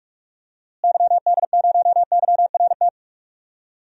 Morse code, heard as a keyed tone, about 46 wpm, 700 Hz. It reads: YD0YRT
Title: DXCC-100 Call Signs - 6 Characters - Repeated 2x - 45wpm